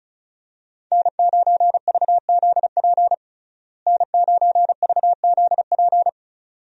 Morse code, heard as a keyed tone, about 34 wpm, 700 Hz. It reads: N9VZP N9VZP
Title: USA Call Signs - 5 Characters - Repeated 2x - 35wpm